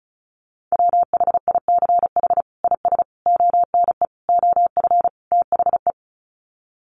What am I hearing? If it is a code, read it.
Morse code, heard as a keyed tone, about 34 wpm, 700 Hz. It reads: WHICH IS ONE OF THE
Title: Sets of 5 Words Encore 35wpm